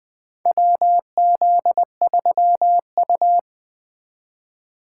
Morse code, heard as a keyed tone, about 20 wpm, 700 Hz. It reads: WZ3U
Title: USA Call Signs - 4 Characters - 2x Character Spacing - 20wpm